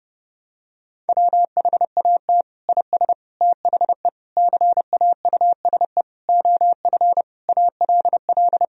Morse code, heard as a keyed tone, about 30 wpm, 700 Hz. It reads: WHAT IS THE CAUSE OF ALL
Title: Sentences from Top 1000 Words - Review Part 1 - 30wpm